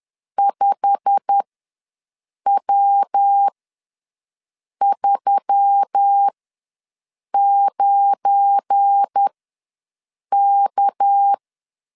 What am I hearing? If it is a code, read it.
Morse code, heard as a keyed tone, about 10 wpm, spaced as for 6 wpm, 800 Hz. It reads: 5W39K
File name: LEZIONI / 8 YJ45 / 8 YJ45 ws 10wpm 3843